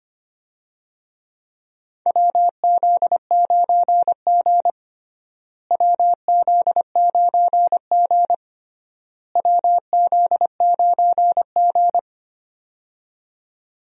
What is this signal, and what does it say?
Morse code, heard as a keyed tone, about 25 wpm, 700 Hz. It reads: WZ9G WZ9G WZ9G
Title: USA Call Signs - 4 Characters - Repeated 3x - 25wpm